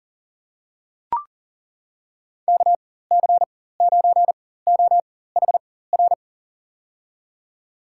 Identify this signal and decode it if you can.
Morse code, heard as a keyed tone, about 40 wpm, spaced as for 19 wpm, 700 Hz. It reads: KC9OHR
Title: USA Call Signs - 6 Characters - 4x Character Spacing - 40wpm